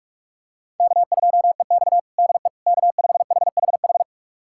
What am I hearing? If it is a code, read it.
Morse code, heard as a keyed tone, about 41 wpm, 700 Hz. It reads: K2EX DE K5HHH
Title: USA Callsign DE Callsign 45wpm